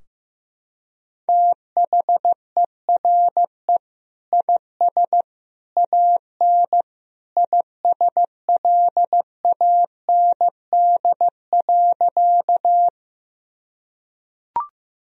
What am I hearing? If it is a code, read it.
Morse code, heard as a keyed tone, about 15 wpm, 700 Hz. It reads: THERE IS AN ISLAND.
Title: Sentences from Top 1000 Words - Review Part 3 - 15wpm